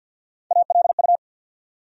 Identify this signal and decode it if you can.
Morse code, heard as a keyed tone, about 46 wpm, 700 Hz. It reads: ALU
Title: USA Call Sign Postfixes - 3 Letters 50wpm